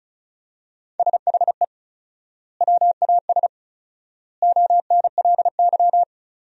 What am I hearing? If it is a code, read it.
Morse code, heard as a keyed tone, about 34 wpm, 700 Hz. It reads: SHE WAS ONLY
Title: 4x Word-Spacing - Sentences from Top 400 Words - 35wpm